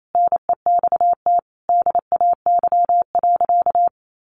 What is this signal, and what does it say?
Morse code, heard as a keyed tone, about 28 wpm, 700 Hz. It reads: NEXT DAY.
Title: Sentences from Top 1200 Words 28wpm